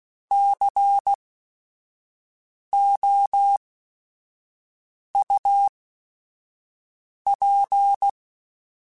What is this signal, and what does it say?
Morse code, heard as a keyed tone, about 16 wpm, spaced as for 5 wpm, 800 Hz. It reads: COUP
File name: Technical Subjects / Technical Subjects-mp3s / Sw Pwr Sup @5